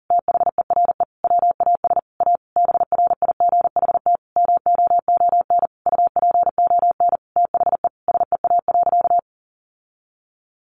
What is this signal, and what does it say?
Morse code, heard as a keyed tone, about 40 wpm, 700 Hz. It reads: THERE WAS A BRIGHT MOON UPON THE SEA.
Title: Sentences from Top 1500 Words 40wpm